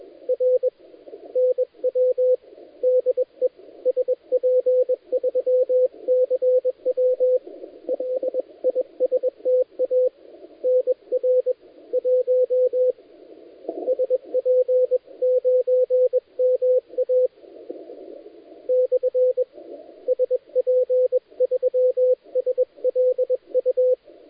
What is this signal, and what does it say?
Morse code, heard as a keyed tone, about 21 wpm, 510 Hz, with radio noise. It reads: R NW DE SP3CW LISTA NR 1 SP9MA / SP3SLU
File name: dlazaawansowanych / nagraniazpasma / titawaNETcontrol